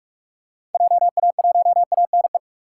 Morse code, heard as a keyed tone, about 41 wpm, 700 Hz. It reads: JA1ANE